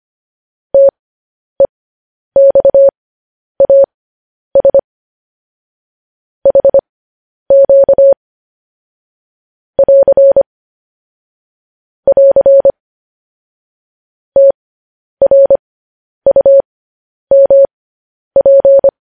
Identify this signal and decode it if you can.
Morse code, heard as a keyed tone, about 25 wpm, spaced as for 10 wpm, 550 Hz. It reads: TEXAS HQ <AR> <AR> TRUMP
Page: News Headlines in Morse Code at 10 WPM